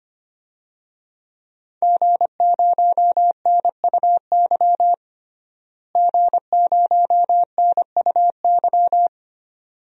Morse code, heard as a keyed tone, about 25 wpm, 700 Hz. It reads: G0NUY G0NUY
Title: DXCC-100 Call Signs - 5 Characters - Repeated 2x - 25wpm